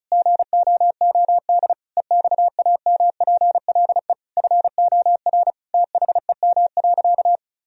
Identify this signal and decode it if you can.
Morse code, heard as a keyed tone, about 36 wpm, 690 Hz. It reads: GOOD EXAMPLE FOR THEM.
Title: Sentences from Top 1000 Words - Review Part 1 - 35wpm